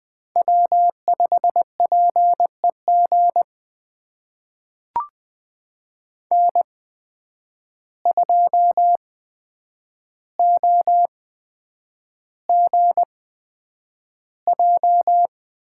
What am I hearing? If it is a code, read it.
Morse code, heard as a keyed tone, about 20 wpm, spaced as for 9 wpm, 700 Hz. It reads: W5PEG N2OGJ
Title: USA Call Signs - 5 Characters - 8x Character Spacing - 20wpm